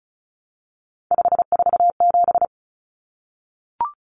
Morse code, heard as a keyed tone, about 34 wpm, 700 Hz. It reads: 547
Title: Numbers - 3-Digit 35wpm